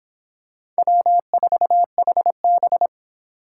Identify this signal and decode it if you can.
Morse code, heard as a keyed tone, about 25 wpm, 700 Hz. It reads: W4HB